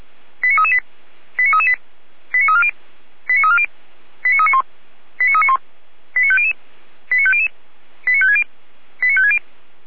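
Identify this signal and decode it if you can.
Five-tone selective call (ZVEI), heard as quick runs of five tones, tones of about 70 ms. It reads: 89338 89338 89340 89340 89391 89391 89500 89500 89560 89560
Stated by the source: ZVEI five-tone selective call 89325, received on SDR.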